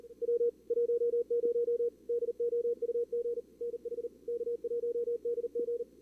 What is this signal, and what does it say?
Morse code, heard as a keyed tone, about 38 wpm, 440 Hz. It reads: W 10 DOUG NH K1DG
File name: CWSprint / Feb2022 / k5tr-feb-na-sprint-cw-k5ot -2300